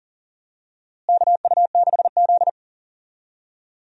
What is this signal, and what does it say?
Morse code, heard as a keyed tone, about 40 wpm, 700 Hz. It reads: KU6Z
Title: USA Call Signs - 4 Characters - Repeated 3x - 40wpm